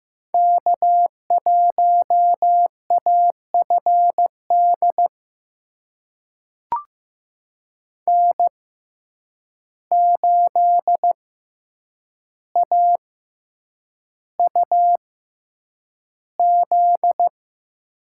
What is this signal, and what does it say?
Morse code, heard as a keyed tone, about 15 wpm, spaced as for 8 wpm, 700 Hz. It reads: K1AFD N8AUZ
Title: USA Call Signs - 5 Characters - 6x Character Spacing - 15wpm